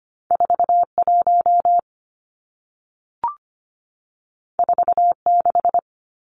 Morse code, heard as a keyed tone, about 25 wpm, 700 Hz. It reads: 41 46
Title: Numbers - 2-Digit 25wpm